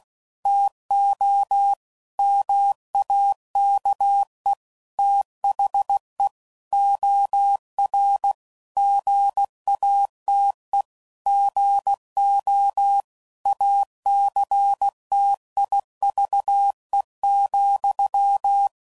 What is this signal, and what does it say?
Morse code, heard as a keyed tone, about 16 wpm, 800 Hz. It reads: TO MAKE THE OR GATE GO ACTIVE,
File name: CW Code Groups @15WPM / Technical Subjects / OR Gate